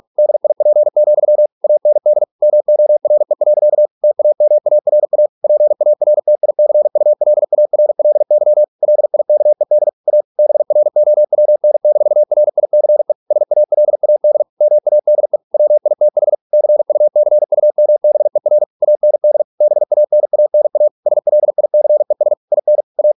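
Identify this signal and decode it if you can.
Morse code, heard as a keyed tone, about 46 wpm, 600 Hz. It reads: DIP, AND MORE. TAMARA PARTICULARLY LIKED A BROWN-RICE SALAD MADE WITH CUCUMBER AND BANANA SLICES IN A
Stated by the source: Morse practice at 45 wpm, the novel Never by Ken Follett in English, from CW med Gnister.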